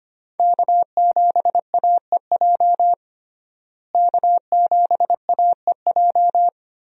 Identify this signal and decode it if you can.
Morse code, heard as a keyed tone, about 25 wpm, 700 Hz. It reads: K7AEJ K7AEJ
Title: USA Call Signs - 5 Characters - Repeated 2x - 25wpm